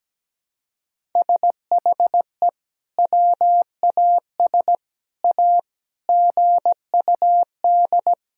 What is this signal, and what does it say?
Morse code, heard as a keyed tone, about 17 wpm, 700 Hz. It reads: SHE WAS A GUD
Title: Sentences from Top 400 Words and Abbreviations 17wpm